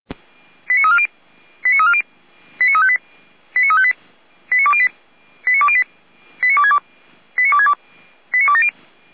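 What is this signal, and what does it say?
Five-tone selective call (ZVEI), heard as quick runs of five tones, tones of about 70 ms. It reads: 89340 89340 89358 89358 89208 89208 89262 89262 89270
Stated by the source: ZVEI five-tone call 89333 received by SDR.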